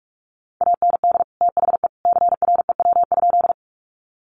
Morse code, heard as a keyed tone, about 44 wpm, 700 Hz. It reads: AND THE CREW?
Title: Sentences from Top 1500 Words 45wpm